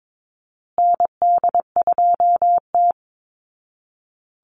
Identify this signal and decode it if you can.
Morse code, heard as a keyed tone, about 22 wpm, 690 Hz. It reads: ND2T
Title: Call Signs Repeated 2x - 22wpm